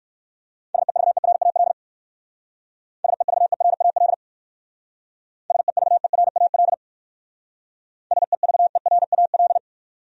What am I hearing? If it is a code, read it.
Morse code, heard as a keyed tone, about 56 wpm, 700 Hz. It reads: SEVERAL SEVERAL SEVERAL SEVERAL
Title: Speed-Racing Top 500 Words 45wpm